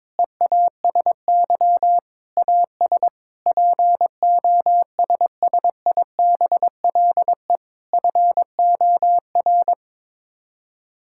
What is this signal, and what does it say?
Morse code, heard as a keyed tone, about 22 wpm, 700 Hz. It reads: EASY AS POSSIBLE FOR
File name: practice / 22wpm / Sets of 8 Words - 22wpm